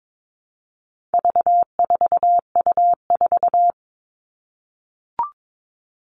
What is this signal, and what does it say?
Morse code, heard as a keyed tone, about 22 wpm, 700 Hz. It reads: V4U4
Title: All Cut Numbers - 4-Digit 22wpm